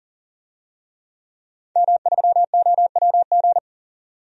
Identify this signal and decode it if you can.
Morse code, heard as a keyed tone, about 40 wpm, 700 Hz. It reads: M3OWG
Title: DXCC-100 Call Signs - 5 Characters - 40wpm